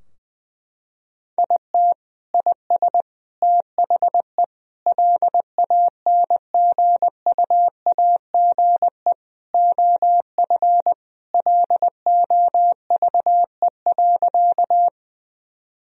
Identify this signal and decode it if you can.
Morse code, heard as a keyed tone, about 19 wpm, 700 Hz. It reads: IT IS THE LANGUAGE OF LOVE.
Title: Sentences from Top 1000 Words - Review Part 3 - 20wpm